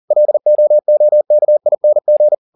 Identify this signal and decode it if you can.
Morse code, heard as a keyed tone, about 40 wpm, 600 Hz. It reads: LOOKING